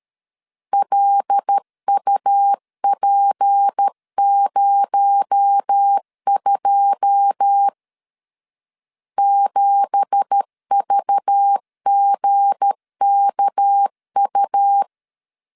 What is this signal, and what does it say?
Morse code, heard as a keyed tone, about 13 wpm, 800 Hz. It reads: LUP02 7VGKU
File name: CW / DET / 1 9 1500C S A 12wpm 3800